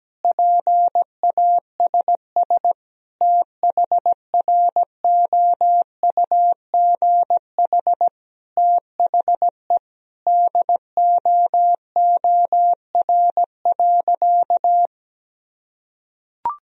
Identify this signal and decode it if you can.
Morse code, heard as a keyed tone, about 17 wpm, 700 Hz. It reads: PASS THROUGH THE DOOR.
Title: Sentences from Top 400 Words and Abbreviations 17wpm